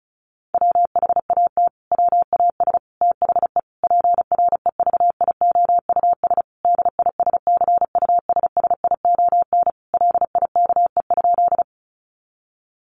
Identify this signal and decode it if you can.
Morse code, heard as a keyed tone, about 34 wpm, 700 Hz. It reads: WHAT WAS THE PREVIOUS DISCUSSION LIKE?
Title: Sentences from Top 1500 Words 35wpm